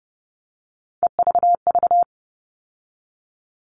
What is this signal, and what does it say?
Morse code, heard as a keyed tone, about 30 wpm, 700 Hz. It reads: EVV
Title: All Cut Numbers - 3-Digit 30wpm